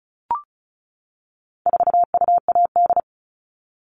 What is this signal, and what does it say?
Morse code, heard as a keyed tone, about 34 wpm, 700 Hz. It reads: 4UAD
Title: All Cut Numbers - 4-Digit 35wpm